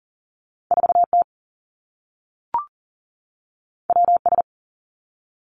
Morse code, heard as a keyed tone, about 38 wpm, 700 Hz. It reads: 4 T W S
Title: Two Challenging Letter-Number Combinations 40wpm